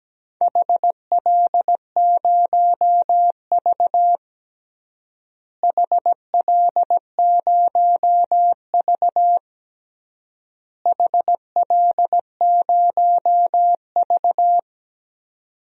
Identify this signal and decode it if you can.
Morse code, heard as a keyed tone, about 17 wpm, 700 Hz. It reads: HL0V HL0V HL0V
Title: DXCC-100 Call Signs - 4 Characters - Repeated 3x - 17wpm